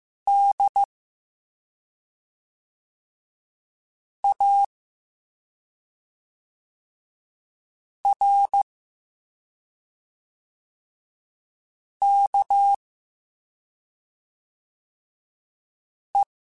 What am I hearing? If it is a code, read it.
Morse code, heard as a keyed tone, about 15 wpm, 800 Hz. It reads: D A R K E